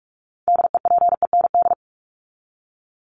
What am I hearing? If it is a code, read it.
Morse code, heard as a keyed tone, about 44 wpm, 700 Hz. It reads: DEPEND
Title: ING Suffix - Top 100 Words with Base Words 45wpm